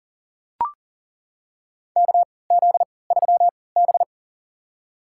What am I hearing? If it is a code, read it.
Morse code, heard as a keyed tone, about 40 wpm, spaced as for 23 wpm, 700 Hz. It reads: KZ3B
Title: USA Call Signs - 4 Characters - 3x Character Spacing - 40wpm